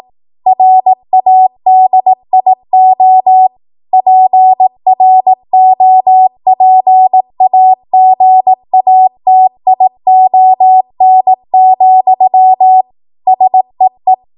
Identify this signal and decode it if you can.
Morse code, heard as a keyed tone, about 18 wpm, 750 Hz. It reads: RADIO PROPAGATION, SEE
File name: Archive / 18 WPM / 18WPM